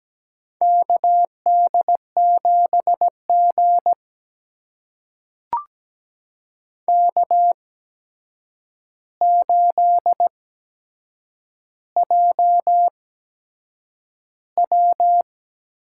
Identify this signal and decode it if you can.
Morse code, heard as a keyed tone, about 17 wpm, spaced as for 7 wpm, 700 Hz. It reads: KD7G K8JW
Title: USA Call Signs - 8x Character Spacing - 17wpm